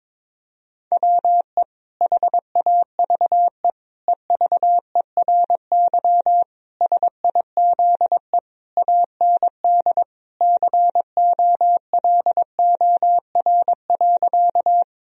Morse code, heard as a keyed tone, about 22 wpm, 700 Hz. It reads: WE HAVE EVERY SIZE AND COLOR.
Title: Sentences from Top 1000 Words - Review Part 2 - 22wpm